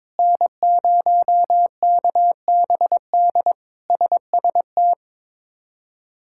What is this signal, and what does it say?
Morse code, heard as a keyed tone, about 22 wpm, 700 Hz. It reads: N0KBD SST
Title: K1USN SST QSO Practice - 22wpm